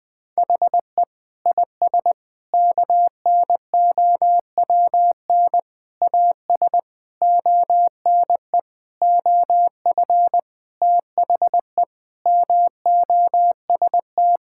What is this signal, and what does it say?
Morse code, heard as a keyed tone, about 20 wpm, 700 Hz. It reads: HE IS KNOWN AS ONE OF THE MOST
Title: Sentences with 5-Word Phrases - Spoken Beginning - 20wpm